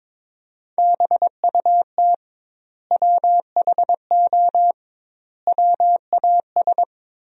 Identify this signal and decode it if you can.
Morse code, heard as a keyed tone, about 22 wpm, 700 Hz. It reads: BUT WHO WAS
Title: 2x Word-Spacing - Sentences from Top 100 Words - 22wpm